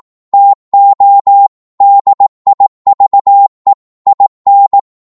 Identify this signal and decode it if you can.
Morse code, heard as a keyed tone, about 18 wpm, 800 Hz. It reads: TO DIVE IN
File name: 18wpm